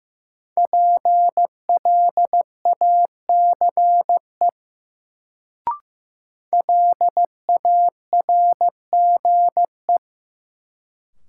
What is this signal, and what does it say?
Morse code, heard as a keyed tone, about 15 wpm, 700 Hz. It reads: PLACE LARGE
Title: Sets of 4 Words - New Word Drill - 15wpm